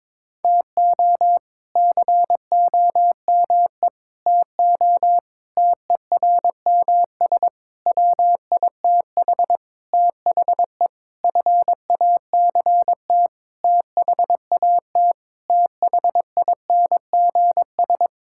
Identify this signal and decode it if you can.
Morse code, heard as a keyed tone, about 21 wpm, 700 Hz. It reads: TO COME TO TERMS WITH THE FACT THAT THINGS
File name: practice / 22wpm / Sentences with 8-Word Phrases - 22wpm